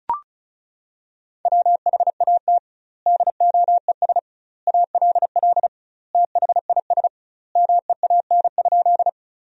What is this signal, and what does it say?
Morse code, heard as a keyed tone, about 36 wpm, 700 Hz. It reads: WHAT DOES ALL THIS MEAN?
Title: 2x Word-Spacing - Sentences from Top 200 Words - 35wpm